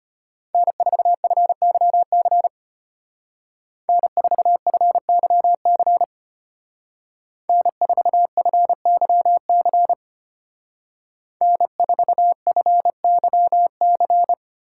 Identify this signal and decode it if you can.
Morse code, heard as a keyed tone, about 33 wpm, 700 Hz. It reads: N4FYC N4FYC N4FYC N4FYC